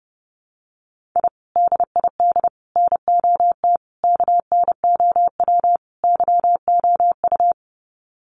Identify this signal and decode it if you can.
Morse code, heard as a keyed tone, about 30 wpm, 700 Hz. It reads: I DID NOT KNOW YOU
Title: Sets of 5 Words Encore2 30wpm